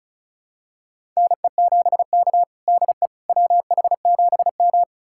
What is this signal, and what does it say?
Morse code, heard as a keyed tone, about 36 wpm, 700 Hz. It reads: NE7K DE WH7M
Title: USA Callsign DE Callsign - 4 Characters - 35wpm